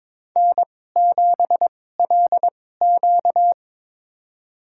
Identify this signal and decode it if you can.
Morse code, heard as a keyed tone, about 22 wpm, spaced as for 16 wpm, 700 Hz. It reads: N7LQ
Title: USA Call Signs - 2x Character Spacing - 22wpm